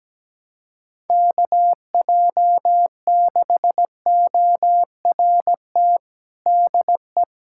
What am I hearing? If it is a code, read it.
Morse code, heard as a keyed tone, about 17 wpm, 700 Hz. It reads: KJ6ORT DE
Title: USA Callsign DE Callsign - 6 Characters - 17wpm